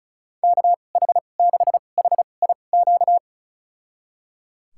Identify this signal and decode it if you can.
Morse code, heard as a keyed tone, about 34 wpm, spaced as for 25 wpm, 700 Hz. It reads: KH6HIQ
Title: USA Call Signs - 6 Characters - 2x Character Spacing - 35wpm